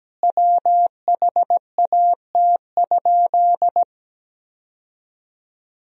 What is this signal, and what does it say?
Morse code, heard as a keyed tone, about 17 wpm, 700 Hz. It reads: WHAT?
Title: 4x Word-Spacing - Sentences from Top 300 Words - 17wpm